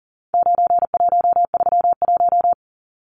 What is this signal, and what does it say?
Morse code, heard as a keyed tone, about 40 wpm, 700 Hz. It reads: 9131
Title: Numbers - 4-Digit 40wpm